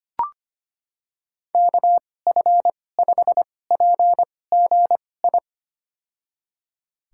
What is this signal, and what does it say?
Morse code, heard as a keyed tone, about 25 wpm, spaced as for 18 wpm, 700 Hz. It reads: KF5PGI
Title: USA Call Signs - 6 Characters - 2x Character Spacing - 25wpm